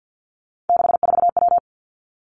Morse code, heard as a keyed tone, about 50 wpm, 700 Hz. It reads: 64W
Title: Three Challenging Letter-Number Combinations 50wpm